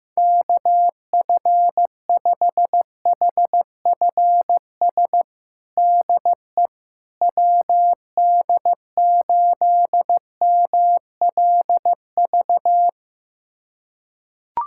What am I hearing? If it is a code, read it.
Morse code, heard as a keyed tone, about 15 wpm, 700 Hz. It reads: KF5HFS DE WD8MLV
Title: USA Callsign DE Callsign - 6 Characters - 15wpm